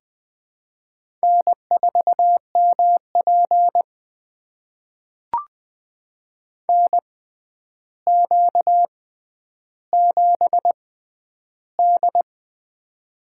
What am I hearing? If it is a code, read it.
Morse code, heard as a keyed tone, about 21 wpm, spaced as for 10 wpm, 700 Hz. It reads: N4MP NQ7D
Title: USA Call Signs - 4 Characters - 6x Character Spacing - 20wpm